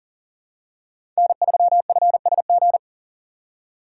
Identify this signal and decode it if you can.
Morse code, heard as a keyed tone, about 40 wpm, 700 Hz. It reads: N3FSG
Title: USA Call Signs - 5 Characters - 4x Character Spacing - 40wpm